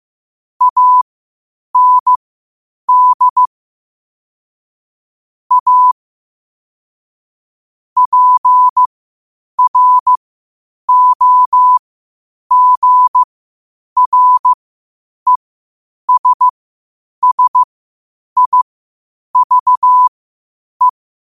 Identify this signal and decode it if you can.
Morse code, heard as a keyed tone, about 15 wpm, spaced as for 8 wpm, 1000 Hz. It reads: AND A PROGRESSIVE
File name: qotdfiles / 8wpm